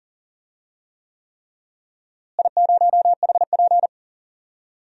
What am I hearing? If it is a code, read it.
Morse code, heard as a keyed tone, about 40 wpm, 700 Hz. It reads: I0HP
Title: DXCC-100 Call Signs - 4 Characters - Repeated 3x - 40wpm